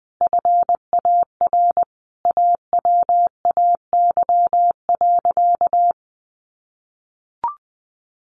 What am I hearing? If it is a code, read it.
Morse code, heard as a keyed tone, about 20 wpm, 700 Hz. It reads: FAR AWAY.
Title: Sentences from Top 1500 Words 20wpm